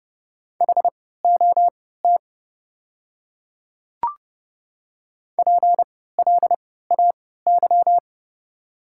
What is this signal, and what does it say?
Morse code, heard as a keyed tone, about 30 wpm, spaced as for 17 wpm, 700 Hz. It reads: HOT PLAY
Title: 3x Character Spacing - Top 200 Words - 30wpm